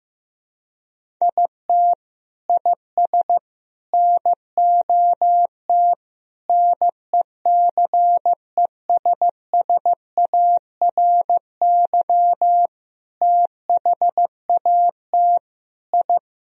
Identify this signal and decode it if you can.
Morse code, heard as a keyed tone, about 15 wpm, 700 Hz. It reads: IT IS NOT NECESSARY THAT I
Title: Sentences from Top 1000 Words - Review Part 1 - 15wpm